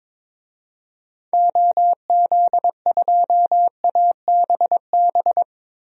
Morse code, heard as a keyed tone, about 22 wpm, 700 Hz. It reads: OZ2ABB
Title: DXCC-100 Call Signs - 6 Characters - Repeated 3x - 22wpm